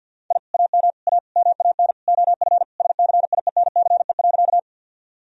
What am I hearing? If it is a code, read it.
Morse code, heard as a keyed tone, about 51 wpm, 700 Hz. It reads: I AM A MAN OF SCIENCE.
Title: Sentences from Top 1000 Words - Review Part 2 - 50wpm